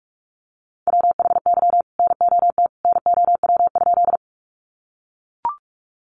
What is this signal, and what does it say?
Morse code, heard as a keyed tone, about 44 wpm, 700 Hz. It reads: WHY NOT NOW?
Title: Sentences from Top 600 Words 45wpm